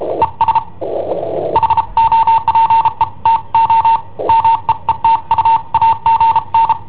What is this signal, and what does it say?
Morse code, heard as a keyed tone, about 33 wpm, 910 Hz. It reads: ES HOPE TO MEET U AGN
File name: i7ohp / cassetta i7ohp B